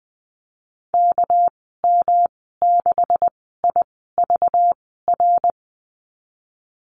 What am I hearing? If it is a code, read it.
Morse code, heard as a keyed tone, about 19 wpm, spaced as for 14 wpm, 700 Hz. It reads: KM6IVR
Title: 213 Lesson 34g - Number 8 - Call Signs 20wpm